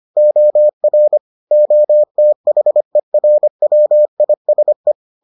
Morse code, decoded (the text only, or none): OR OTHERWISE